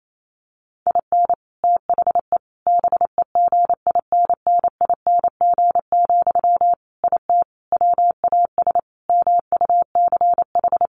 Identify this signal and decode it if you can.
Morse code, heard as a keyed tone, about 28 wpm, 700 Hz. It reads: IN THE BEGINNING, IT WAS MUCH
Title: Sentences from Top 1400 Words 28wpm